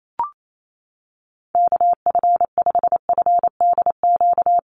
Morse code, heard as a keyed tone, about 28 wpm, 700 Hz. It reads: KF5FDQ
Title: USA Call Signs Repeated 2x 28wpm